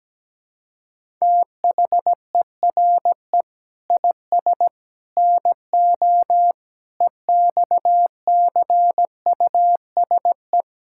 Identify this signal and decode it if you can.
Morse code, heard as a keyed tone, about 17 wpm, 700 Hz. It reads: THERE IS NO EXCUSE
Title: Sentences from Top 1000 Words - Review Part 6 - 17wpm